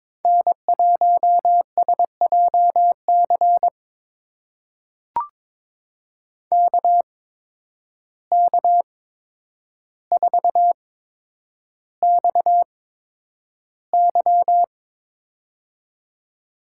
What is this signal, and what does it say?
Morse code, heard as a keyed tone, about 22 wpm, spaced as for 9 wpm, 700 Hz. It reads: N1SJC KK4XY
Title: USA Call Signs - 5 Characters - 8x Character Spacing - 22wpm